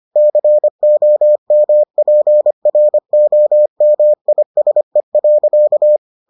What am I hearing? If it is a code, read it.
Morse code, heard as a keyed tone, about 25 wpm, 600 Hz. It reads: COMPROMISE.